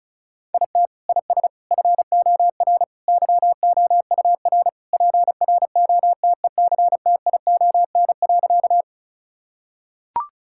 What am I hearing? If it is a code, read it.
Morse code, heard as a keyed tone, about 34 wpm, 700 Hz. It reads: IT IS FOR YOUR PROTECTION.
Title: Sentences from Top 1600 Words 35wpm